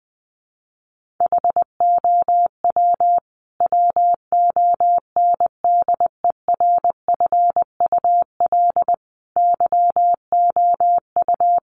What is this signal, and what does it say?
Morse code, heard as a keyed tone, about 20 wpm, 700 Hz. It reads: HOW WONDERFUL YOU
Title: Sentences from Top 1100 Words 20wpm